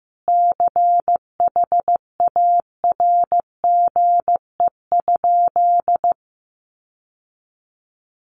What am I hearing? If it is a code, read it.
Morse code, heard as a keyed tone, about 15 wpm, 700 Hz. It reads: CHARGE?
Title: Sentences from Top 900 Words 15wpm